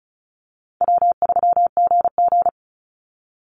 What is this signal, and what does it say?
Morse code, heard as a keyed tone, about 34 wpm, 700 Hz. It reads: W3GG
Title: Call Signs - 35wpm